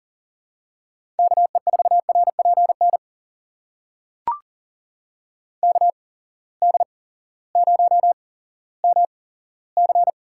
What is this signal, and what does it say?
Morse code, heard as a keyed tone, about 38 wpm, spaced as for 18 wpm, 700 Hz. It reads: KE4RPN KD0MC
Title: USA Call Signs - 6 Characters - 8x Character Spacing - 40wpm